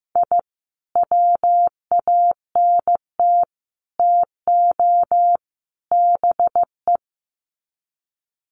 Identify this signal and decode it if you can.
Morse code, heard as a keyed tone, about 15 wpm, 700 Hz. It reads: I WANT TO BE
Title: Sets of 4 Words 15wpm